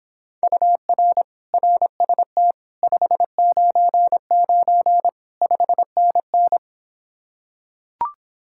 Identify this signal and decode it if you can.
Morse code, heard as a keyed tone, about 26 wpm, 700 Hz. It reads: UR RST 599 5NN